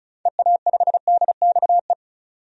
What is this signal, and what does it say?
Morse code, heard as a keyed tone, about 34 wpm, 700 Hz. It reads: EA5DXE
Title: DXCC-100 Call Signs - 6 Characters - 35wpm